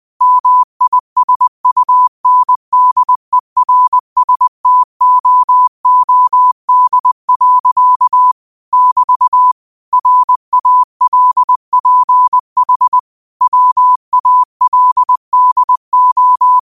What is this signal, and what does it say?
Morse code, heard as a keyed tone, about 20 wpm, 1000 Hz. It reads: MISUNDERSTOOD. <BT> RALPH WALDO
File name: qotdfiles / 20wpm